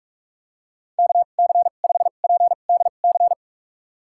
Morse code, heard as a keyed tone, about 46 wpm, spaced as for 32 wpm, 700 Hz. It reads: KC5PDC
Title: USA Call Signs - 6 Characters - 2x Character Spacing - 45wpm